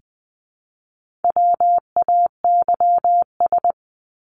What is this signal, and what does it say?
Morse code, heard as a keyed tone, about 20 wpm, 700 Hz. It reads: WAYS
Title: ES:S Suffix - Top 100 Words with Base Words 20wpm